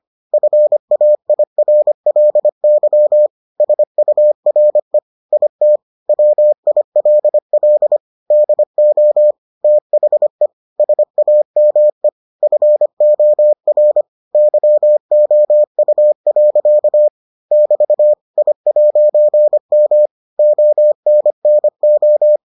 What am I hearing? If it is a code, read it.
Morse code, heard as a keyed tone, about 25 wpm, 600 Hz. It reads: FAIRLY SURE IT WILL DO THE SAME FOR YOU. <BT> I'M ONNO